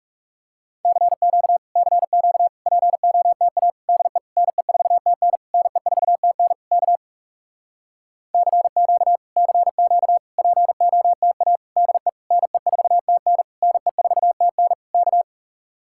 Morse code, heard as a keyed tone, about 37 wpm, 700 Hz. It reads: CQ CQ POTA DE NE4TN NE4TN K CQ CQ POTA DE NE4TN NE4TN K
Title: Speed-Racing Common QSO Phrases Encore - 30wpm